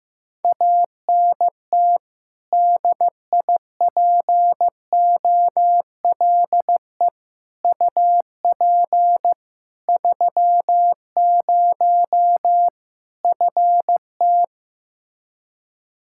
Morse code, heard as a keyed tone, about 15 wpm, 700 Hz. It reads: ANT DIPOLE UP 30 FT
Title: K1USN SST QSO Practice 15wpm